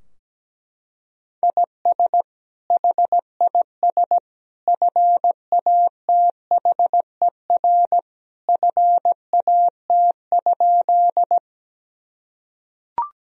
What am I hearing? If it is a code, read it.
Morse code, heard as a keyed tone, about 17 wpm, 700 Hz. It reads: IS HIS FATHER FAT?
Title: Sentences from Top 1000 Words - Review Part 4 - 17wpm